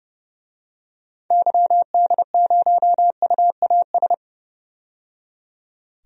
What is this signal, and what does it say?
Morse code, heard as a keyed tone, about 30 wpm, 700 Hz. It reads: YD0UAS
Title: DXCC-100 Call Signs - 6 Characters - 30wpm